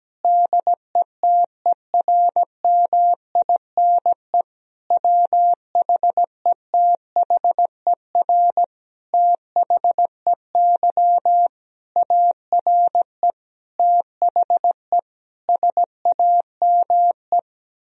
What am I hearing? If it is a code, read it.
Morse code, heard as a keyed tone, about 17 wpm, 700 Hz. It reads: DETERMINE WHETHER THEY ARE THE SAME
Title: Sentences from Top 1000 Words - Review Part 4 - 17wpm